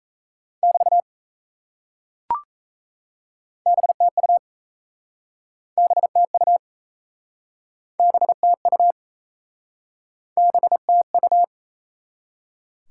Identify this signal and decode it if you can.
Morse code, heard as a keyed tone, about 37 wpm, 700 Hz. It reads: <BT> BTU BTU BTU BTU